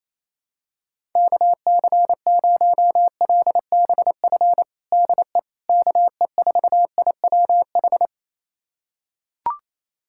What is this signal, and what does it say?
Morse code, heard as a keyed tone, about 28 wpm, 700 Hz. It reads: KC0LBF DE KE4IWH
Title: USA Callsign DE Callsign - 6 Characters - 28wpm